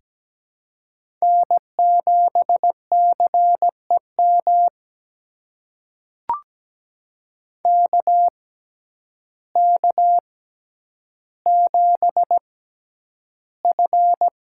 Morse code, heard as a keyed tone, about 17 wpm, spaced as for 9 wpm, 700 Hz. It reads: N7CEM KK7F
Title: USA Call Signs - 5 Characters - 6x Character Spacing - 17wpm